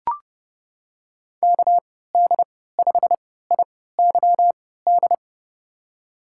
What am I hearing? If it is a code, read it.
Morse code, heard as a keyed tone, about 30 wpm, spaced as for 17 wpm, 700 Hz. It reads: KD5IYD
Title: USA Call Signs - 6 Characters - 3x Character Spacing - 30wpm